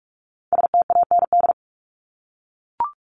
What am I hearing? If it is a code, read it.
Morse code, heard as a keyed tone, about 44 wpm, 700 Hz. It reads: STAND
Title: Top 500 Words 45wpm